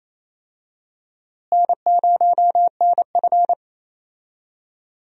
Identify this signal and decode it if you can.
Morse code, heard as a keyed tone, about 28 wpm, 700 Hz. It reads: N0NF
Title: USA Call Signs - 4 Characters - Repeated 3x - 28wpm